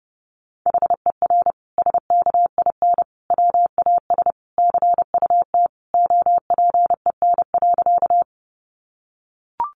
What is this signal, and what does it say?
Morse code, heard as a keyed tone, about 30 wpm, 700 Hz. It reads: HER SKIN WAS CUT OPEN.
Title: Sentences from Top 1300 Words 30wpm